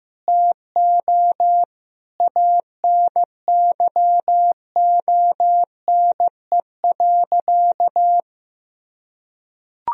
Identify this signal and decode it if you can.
Morse code, heard as a keyed tone, about 15 wpm, 700 Hz. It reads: TO ANYONE.
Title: Sentences from Top 1000 Words - Review Part 2 - 15wpm